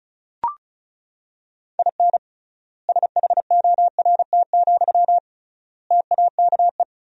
Morse code, heard as a keyed tone, about 34 wpm, 700 Hz. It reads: IN SHORT, TAKE
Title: 3x Word-Spacing - Sentences from Top 400 Words - 35wpm